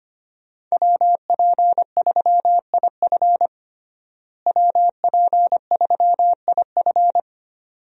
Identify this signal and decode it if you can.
Morse code, heard as a keyed tone, about 25 wpm, 700 Hz. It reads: WP3IF WP3IF
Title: USA Call Signs - 5 Characters - Repeated 2x - 25wpm